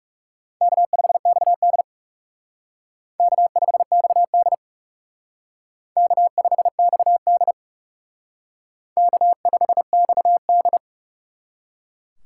Morse code, heard as a keyed tone, about 37 wpm, 700 Hz. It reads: K5XD K5XD K5XD K5XD
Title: Speed Racing - USA Call Signs 30wpm